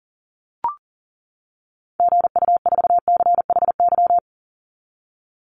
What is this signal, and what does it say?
Morse code, heard as a keyed tone, about 40 wpm, 700 Hz. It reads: GU4CHY
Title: Call Signs - 40wpm